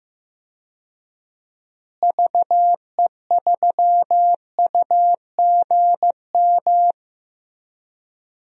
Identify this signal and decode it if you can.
Morse code, heard as a keyed tone, about 15 wpm, 700 Hz. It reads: VE3UGM
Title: DXCC-100 Call Signs - 6 Characters - Repeated 2x - 15wpm